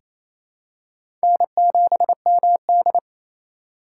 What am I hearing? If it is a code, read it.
Morse code, heard as a keyed tone, about 29 wpm, 700 Hz. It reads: N7MD
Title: USA Call Signs - 4 Characters - 4x Character Spacing - 28wpm